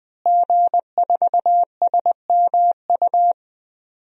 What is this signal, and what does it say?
Morse code, heard as a keyed tone, about 21 wpm, 700 Hz. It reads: G4SMU
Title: DXCC-100 Call Signs - 5 Characters - Repeated 3x - 20wpm